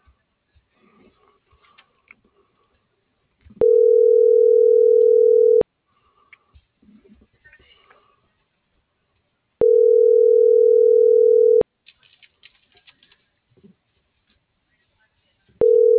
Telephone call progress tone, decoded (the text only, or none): ringback tone